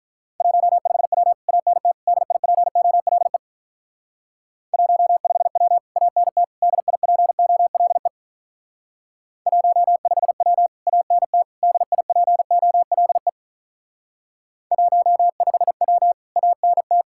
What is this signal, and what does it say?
Morse code, heard as a keyed tone, about 43 wpm, 700 Hz. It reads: 15W ANT DIPOLE 15W ANT DIPOLE 15W ANT DIPOLE 15W ANT
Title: Speed-Racing Common QSO Phrases - 35wpm